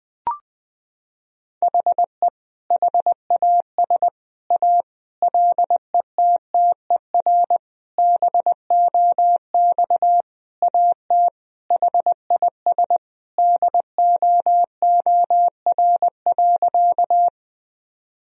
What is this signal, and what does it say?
Morse code, heard as a keyed tone, about 21 wpm, 700 Hz. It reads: HE HAS A LETTER BOX AT HIS DOOR.
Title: Sentences from Top 1000 Words - Review Part 2 - 20wpm